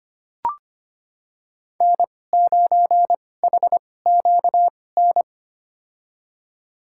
Morse code, heard as a keyed tone, about 25 wpm, spaced as for 18 wpm, 700 Hz. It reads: N9HQN
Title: USA Call Signs - 5 Characters - 2x Character Spacing - 25wpm